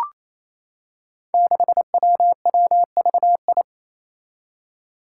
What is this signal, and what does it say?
Morse code, heard as a keyed tone, about 28 wpm, 700 Hz. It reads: 6WWVI
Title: Five Challenging Letter-Number Combinations 28wpm